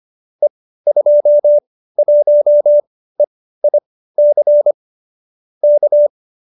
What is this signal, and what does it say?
Morse code, heard as a keyed tone, about 25 wpm, spaced as for 15 wpm, 570 Hz. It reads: E21EIC K